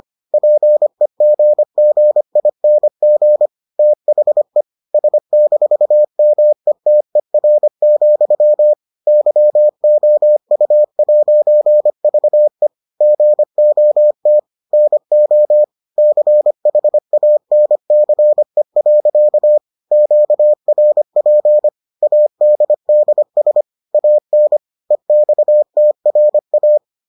Morse code, heard as a keyed tone, about 25 wpm, 600 Hz. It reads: PEGGING THE S-METER, YOU'VE GOT NO CHANCE. QRP ADDS AN EXTRA